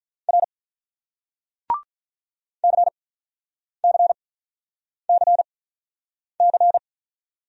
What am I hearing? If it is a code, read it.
Morse code, heard as a keyed tone, about 48 wpm, spaced as for 9 wpm, 700 Hz. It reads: R CCCC